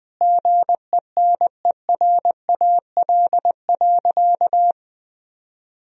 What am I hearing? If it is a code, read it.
Morse code, heard as a keyed tone, about 21 wpm, 700 Hz. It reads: GENERAL.
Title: Sentences from Top 1000 Words - Review Part 3 - 20wpm